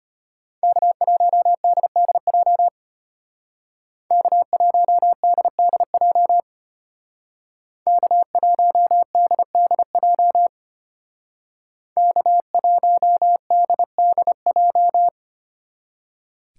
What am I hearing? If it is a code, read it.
Morse code, heard as a keyed tone, about 32 wpm, 700 Hz. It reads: K1DDJ K1DDJ K1DDJ K1DDJ